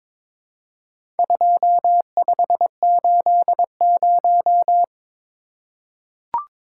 Morse code, heard as a keyed tone, about 22 wpm, 700 Hz. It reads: 2580
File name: practice / 22wpm / Four Letter-Number Combinations - 22wpm